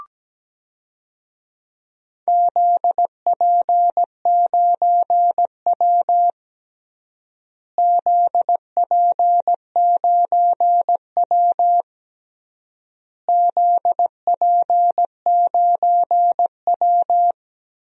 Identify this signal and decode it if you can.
Morse code, heard as a keyed tone, about 17 wpm, 700 Hz. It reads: ZP9W ZP9W ZP9W
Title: DXCC-100 Call Signs - 4 Characters - Repeated 3x - 17wpm